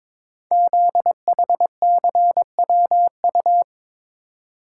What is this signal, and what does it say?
Morse code, heard as a keyed tone, about 22 wpm, 700 Hz. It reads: ZHCWU
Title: Code Groups - 5 Letters - 22wpm